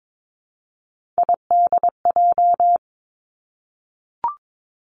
Morse code, heard as a keyed tone, about 22 wpm, 700 Hz. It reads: IDJ
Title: Three Challenging Letter-Number Combinations 22wpm